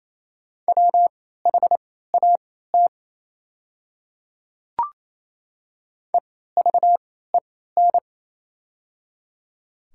Morse code, heard as a keyed tone, about 28 wpm, spaced as for 16 wpm, 700 Hz. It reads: WHAT EVEN
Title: 3x Character Spacing - Top 200 Words - 28wpm